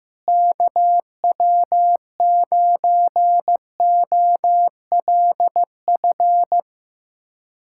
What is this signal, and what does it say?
Morse code, heard as a keyed tone, about 15 wpm, 700 Hz. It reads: KW9OLF
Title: USA Callsign DE Callsign 15wpm